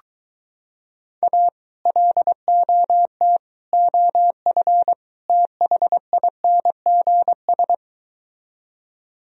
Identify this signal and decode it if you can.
Morse code, heard as a keyed tone, about 23 wpm, 700 Hz. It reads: A LOT OF THINGS